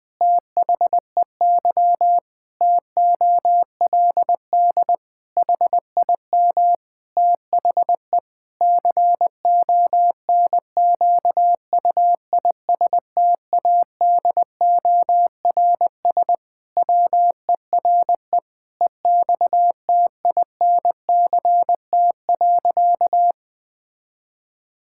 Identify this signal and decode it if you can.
Morse code, heard as a keyed tone, about 19 wpm, 700 Hz. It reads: THEY TOLD HIM THE CONQUISTADORS WERE EXTINCT.
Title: Pangram Challenge v2 - 20wpm